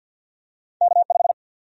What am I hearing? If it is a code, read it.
Morse code, heard as a keyed tone, about 52 wpm, 700 Hz. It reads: K5
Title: USA Call Sign Prefixes - 50wpm